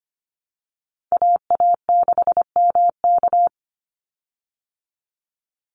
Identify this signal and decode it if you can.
Morse code, heard as a keyed tone, about 25 wpm, 700 Hz. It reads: AA6MK
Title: Call Signs Repeated 2x - 25wpm